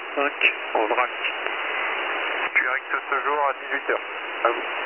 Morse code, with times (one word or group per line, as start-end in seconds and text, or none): none